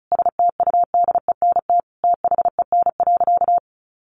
Morse code, heard as a keyed tone, about 34 wpm, 700 Hz. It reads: STUDENT THEN.
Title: Sentences from Top 1000 Words 35wpm